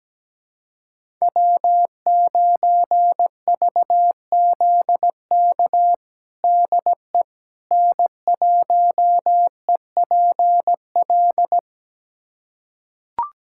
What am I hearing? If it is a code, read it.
Morse code, heard as a keyed tone, about 17 wpm, 700 Hz. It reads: W9VZK DE N1EPL
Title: USA Callsign DE Callsign - 5 Characters - 17wpm